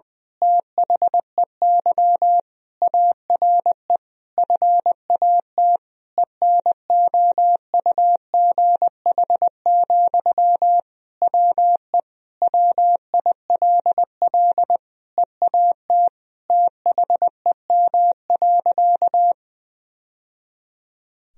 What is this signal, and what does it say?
Morse code, heard as a keyed tone, about 20 wpm, 700 Hz. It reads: THEY ARE FAT ENOUGH, WE WILL EAT THEM.
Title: Sentences from Top 1000 Words - Review Part 4 - 20wpm